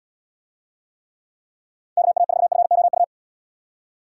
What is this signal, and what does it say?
Morse code, heard as a keyed tone, about 70 wpm, 700 Hz. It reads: KA4FOU